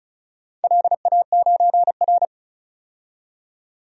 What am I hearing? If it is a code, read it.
Morse code, heard as a keyed tone, about 34 wpm, 700 Hz. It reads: LA9R
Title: DXCC-100 Call Signs - 4 Characters - 35wpm